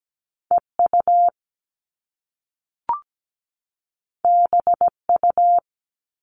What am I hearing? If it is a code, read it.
Morse code, heard as a keyed tone, about 17 wpm, 700 Hz. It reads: EU BU
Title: All Cut Numbers - 2-Digit 17wpm